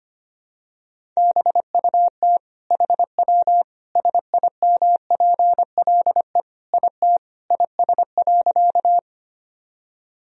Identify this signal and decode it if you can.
Morse code, heard as a keyed tone, about 25 wpm, 700 Hz. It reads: BUT HW SIMPLE IT IS.
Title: Sentences from Top 400 Words and Abbreviations 25wpm